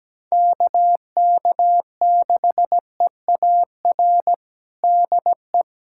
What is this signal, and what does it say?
Morse code, heard as a keyed tone, about 17 wpm, 700 Hz. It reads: KK6EAR DE
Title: USA Callsign DE Callsign 17wpm